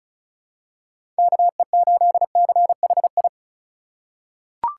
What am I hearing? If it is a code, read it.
Morse code, heard as a keyed tone, about 34 wpm, 700 Hz. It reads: KE8CHI
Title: USA Call Signs - 4x Character Spacing - 35wpm